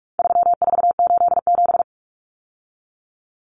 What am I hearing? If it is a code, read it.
Morse code, heard as a keyed tone, about 44 wpm, 700 Hz. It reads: 3487
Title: Numbers - 4-Digit 45wpm